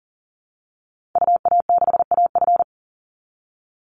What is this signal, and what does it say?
Morse code, heard as a keyed tone, about 40 wpm, 700 Hz. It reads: UA6AF
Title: Call Signs Repeated 3x - 40wpm